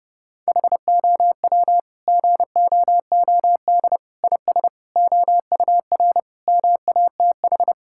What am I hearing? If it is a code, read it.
Morse code, heard as a keyed tone, about 32 wpm, 700 Hz. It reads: HOW GOOD IS OUR MATH
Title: Sentences from Top 1000 Words - Review Part 6 - 30wpm